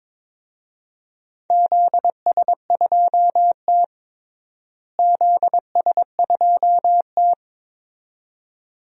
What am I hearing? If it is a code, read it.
Morse code, heard as a keyed tone, about 22 wpm, 700 Hz. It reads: ZS2T ZS2T
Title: DXCC-100 Call Signs - 4 Characters - Repeated 2x - 22wpm